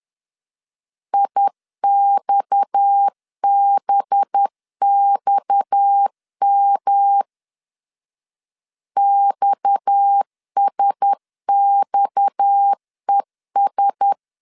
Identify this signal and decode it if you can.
Morse code, heard as a keyed tone, about 10 wpm, 800 Hz. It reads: IXBXM XSXES
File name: LEZIONI / 6 PKX0 / 6 PKX 10wpm 2426